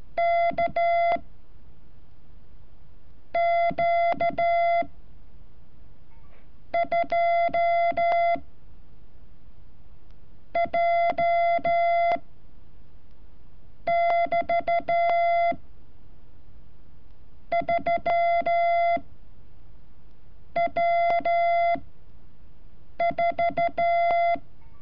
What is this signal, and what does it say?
Morse code, heard as a keyed tone, about 14 wpm, spaced as for 6 wpm, 680 Hz. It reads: K Q 2 J <BT> 3 W4
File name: curso / ejercicio 31